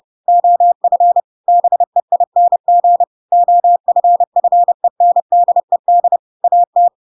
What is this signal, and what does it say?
Morse code, heard as a keyed tone, about 30 wpm, 700 Hz. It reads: OF BEING OFFENDED AT
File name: Pinocchio / Pinocchio ch4 30 30 ik1hge